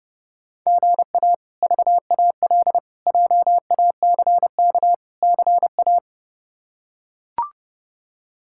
Morse code, heard as a keyed tone, about 29 wpm, 700 Hz. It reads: GA VAL JACK CA
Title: K1USN SST QSO Practice - 30wpm